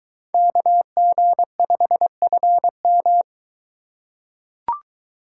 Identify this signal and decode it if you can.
Morse code, heard as a keyed tone, about 22 wpm, 700 Hz. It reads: KG5FM